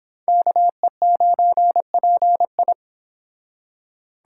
Morse code, heard as a keyed tone, about 30 wpm, 700 Hz. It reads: KE9PI